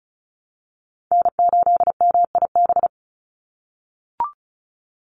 Morse code, heard as a keyed tone, about 34 wpm, 700 Hz. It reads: N8MIB
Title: USA Call Signs 35wpm